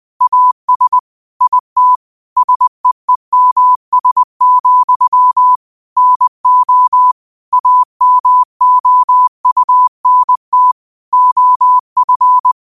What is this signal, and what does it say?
Morse code, heard as a keyed tone, about 20 wpm, 1000 Hz. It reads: AS IT SEEMS, NO AMOUNT OF
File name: qotdfiles / 20wpm